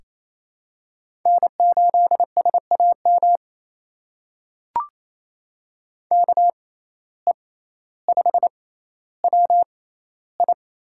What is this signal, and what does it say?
Morse code, heard as a keyed tone, about 28 wpm, spaced as for 14 wpm, 700 Hz. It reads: N8SAM KE5WI
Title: USA Call Signs - 5 Characters - 6x Character Spacing - 28wpm